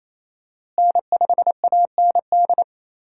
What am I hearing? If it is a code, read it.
Morse code, heard as a keyed tone, about 28 wpm, 700 Hz. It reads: N5AND
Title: USA Call Signs - 5 Characters - Repeated 2x - 28wpm